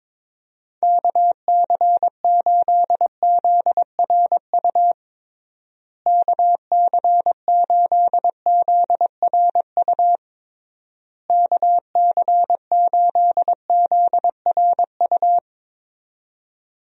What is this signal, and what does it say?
Morse code, heard as a keyed tone, about 22 wpm, 700 Hz. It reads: KC8ZRU KC8ZRU KC8ZRU
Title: USA Call Signs - 6 Characters - Repeated 3x - 22wpm